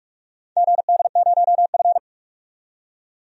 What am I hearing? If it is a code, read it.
Morse code, heard as a keyed tone, about 45 wpm, 700 Hz. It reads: GD0F
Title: DXCC-100 Call Signs - 4 Characters - 45wpm